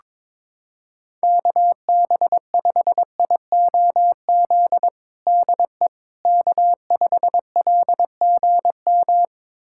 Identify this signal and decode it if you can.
Morse code, heard as a keyed tone, about 22 wpm, 700 Hz. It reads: KB5IOZ DE K5LGM
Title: USA Callsign DE Callsign 22wpm